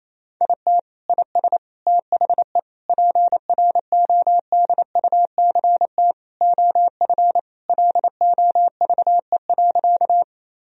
Morse code, heard as a keyed tone, about 28 wpm, 700 Hz. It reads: IT IS THE PRODUCT OF LOVE.
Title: Sentences from Top 1000 Words - Review Part 2 - 28wpm